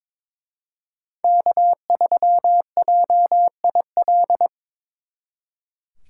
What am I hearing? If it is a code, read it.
Morse code, heard as a keyed tone, about 26 wpm, 700 Hz. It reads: K3JIL
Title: Speed Racing - USA 1x3 Call Signs 22wpm